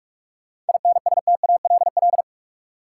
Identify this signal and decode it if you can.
Morse code, heard as a keyed tone, about 46 wpm, 700 Hz. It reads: INSTALL
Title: Sentences from Top 1600 Words - New Word Drill - 45wpm